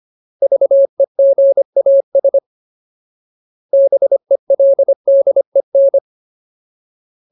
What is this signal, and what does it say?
Morse code, heard as a keyed tone, about 25 wpm, 550 Hz. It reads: VEGAS BELDEN
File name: Ultimate Word List / 25WPM Ultimate